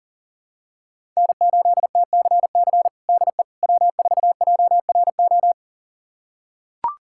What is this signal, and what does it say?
Morse code, heard as a keyed tone, about 37 wpm, 700 Hz. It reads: N8TCC DE W4JRO
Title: USA Callsign DE Callsign - 5 Characters - 40wpm